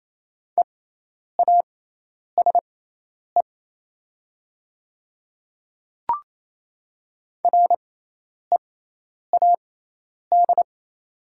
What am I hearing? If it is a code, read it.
Morse code, heard as a keyed tone, about 28 wpm, spaced as for 10 wpm, 700 Hz. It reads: EASE READ
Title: 6x Character Spacing - Top 300 Words - 28wpm